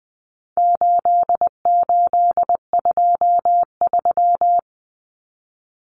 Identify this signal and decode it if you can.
Morse code, heard as a keyed tone, about 20 wpm, 700 Hz. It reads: 8823
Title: Numbers - 4-Digit 20wpm